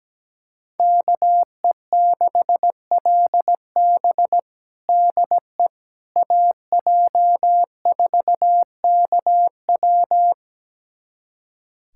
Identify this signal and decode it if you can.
Morse code, heard as a keyed tone, about 17 wpm, 700 Hz. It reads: KE6LB DE AJ4KW
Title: USA Callsign DE Callsign - 5 Characters - 17wpm